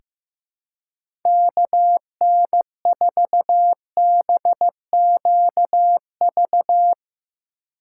Morse code, heard as a keyed tone, about 15 wpm, 700 Hz. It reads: KN4BQV
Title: USA Call Signs - 6 Characters - 4x Character Spacing - 15wpm